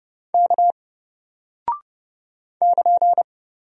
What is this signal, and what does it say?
Morse code, heard as a keyed tone, about 30 wpm, 700 Hz. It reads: K <KN>
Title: Single Character (Letter:Number:Punctuation:Prosign) Alt 30wpm